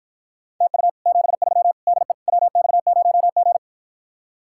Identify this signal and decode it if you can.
Morse code, heard as a keyed tone, about 53 wpm, 700 Hz. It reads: TU 73 DE WX0G